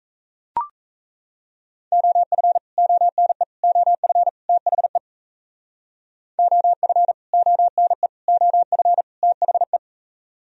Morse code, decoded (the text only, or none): OF ONE OF THE OF ONE OF THE